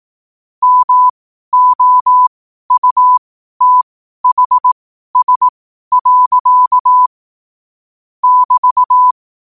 Morse code, heard as a keyed tone, about 18 wpm, spaced as for 12 wpm, 1000 Hz. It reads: MOUTHS. <BT>
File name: qotdfiles / 12wpm